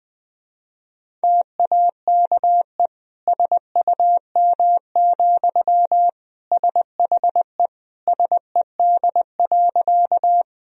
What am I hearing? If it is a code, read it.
Morse code, heard as a keyed tone, about 20 wpm, 700 Hz. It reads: TAKE SUM, SHE SED.
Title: Sentences from Top 200 Words and Abbreviations 20wpm